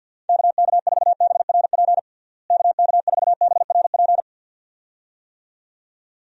Morse code, heard as a keyed tone, about 47 wpm, 700 Hz. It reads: KK4BRP KK4BRP
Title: USA Call Signs - 6 Characters - Repeated 2x - 50wpm